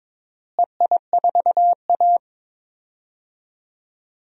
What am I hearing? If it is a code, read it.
Morse code, heard as a keyed tone, about 21 wpm, 700 Hz. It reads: EI4A
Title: DXCC-100 Call Signs - 4 Characters - Repeated 2x - 22wpm